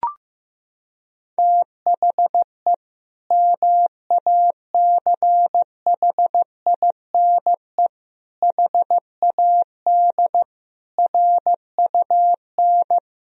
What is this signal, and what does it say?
Morse code, heard as a keyed tone, about 16 wpm, 700 Hz. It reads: THE MACHINE HAD RUN
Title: Sentences from Top 1000 Words - Review Part 2 - 15wpm